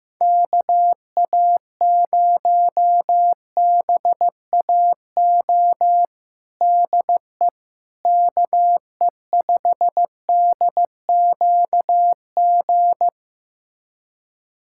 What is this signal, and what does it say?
Morse code, heard as a keyed tone, about 16 wpm, 700 Hz. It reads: KA0BAO DE KE5DQG
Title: USA Callsign DE Callsign 15wpm